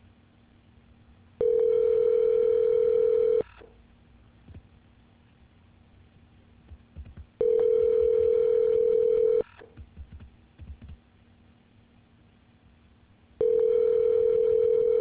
Ringback tone (North America), 440 + 480 Hz.